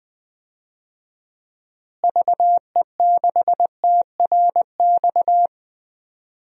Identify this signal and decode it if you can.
Morse code, heard as a keyed tone, about 20 wpm, 700 Hz. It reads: VE6TRX
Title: DXCC-100 Call Signs - 6 Characters - Repeated 2x - 20wpm